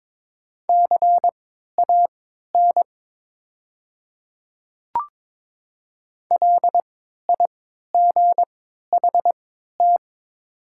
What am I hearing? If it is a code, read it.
Morse code, heard as a keyed tone, about 22 wpm, spaced as for 12 wpm, 700 Hz. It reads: CAN LIGHT
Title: 3x Character Spacing - Top 200 Words - 22wpm